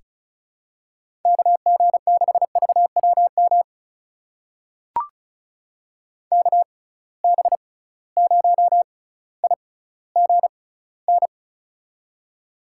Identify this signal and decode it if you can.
Morse code, heard as a keyed tone, about 34 wpm, spaced as for 18 wpm, 700 Hz. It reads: KG6VWM KB0IGN
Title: USA Call Signs - 6 Characters - 6x Character Spacing - 35wpm